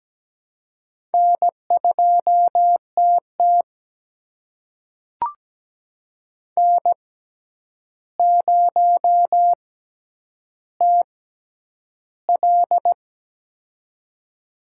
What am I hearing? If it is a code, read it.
Morse code, heard as a keyed tone, about 17 wpm, spaced as for 9 wpm, 700 Hz. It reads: N2TT N0TL
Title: USA Call Signs - 4 Characters - 6x Character Spacing - 17wpm